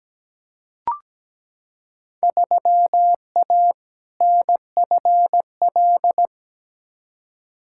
Morse code, heard as a keyed tone, about 17 wpm, 700 Hz. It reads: 3A NFL
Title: Field Day Exchanges - 17wpm